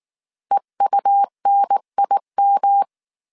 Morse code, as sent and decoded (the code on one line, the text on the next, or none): . ..- -. .. --
EUNIM